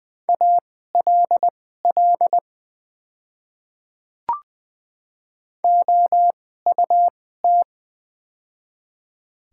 Morse code, heard as a keyed tone, about 20 wpm, spaced as for 14 wpm, 700 Hz. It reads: ALL OUT
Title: 2x Character Spacing - Top 300 Words - 20wpm